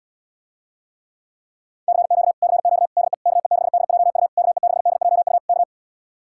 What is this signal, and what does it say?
Morse code, heard as a keyed tone, about 77 wpm, 700 Hz. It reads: CQ CQ DE KE6DJU KE6DJU K